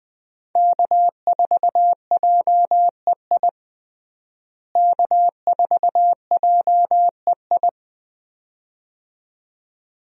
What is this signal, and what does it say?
Morse code, heard as a keyed tone, about 20 wpm, 700 Hz. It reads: K4JEI K4JEI
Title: USA Call Signs - 5 Characters - Repeated 2x - 20wpm